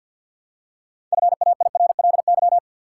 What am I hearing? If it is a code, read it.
Morse code, heard as a keyed tone, about 50 wpm, 700 Hz. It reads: FAIRLY
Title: Six Letter Words - 50wpm